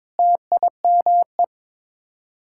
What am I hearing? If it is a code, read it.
Morse code, heard as a keyed tone, about 22 wpm, 700 Hz. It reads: TIME
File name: practice / 22wpm / Sets of 8 Words - 22wpm